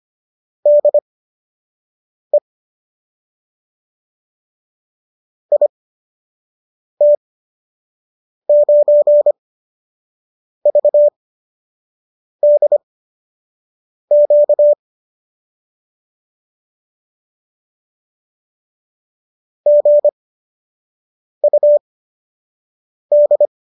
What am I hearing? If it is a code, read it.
Morse code, heard as a keyed tone, about 25 wpm, spaced as for 6 wpm, 590 Hz. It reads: DE IT9VDQ GUD